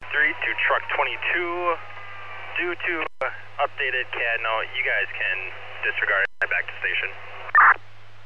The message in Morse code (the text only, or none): none